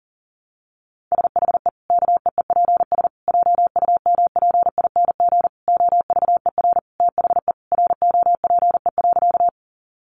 Morse code, heard as a keyed tone, about 40 wpm, 700 Hz. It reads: SHE KEEPS JUMPING OVER THE ROPE.
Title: Easy Phrases 40wpm